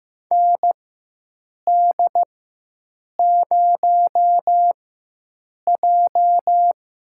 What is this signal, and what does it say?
Morse code, heard as a keyed tone, about 15 wpm, spaced as for 7 wpm, 700 Hz. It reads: ND0J
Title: USA Call Signs - 4 Characters - 4x Character Spacing - 15wpm